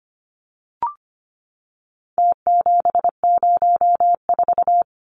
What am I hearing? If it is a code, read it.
Morse code, heard as a keyed tone, about 25 wpm, 700 Hz. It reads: T704
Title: Numbers - Contest Style - 4-Digit 25wpm